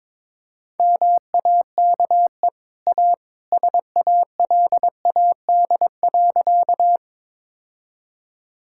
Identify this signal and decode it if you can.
Morse code, heard as a keyed tone, about 22 wpm, 690 Hz. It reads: MAKE A SALAD.
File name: practice / 22wpm / Easy Sentences with Hints - 22wpm